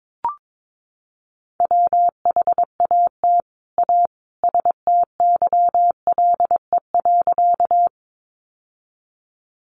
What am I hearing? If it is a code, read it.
Morse code, heard as a keyed tone, about 22 wpm, 700 Hz. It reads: WHAT A STYLE.
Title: Sentences from Top 800 Words 22wpm